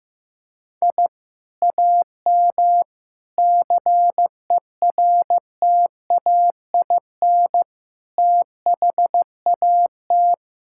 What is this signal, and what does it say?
Morse code, heard as a keyed tone, about 15 wpm, 700 Hz. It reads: I AM CERTAIN THAT
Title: Sentences from Top 1000 Words - Review Part 6 - 15wpm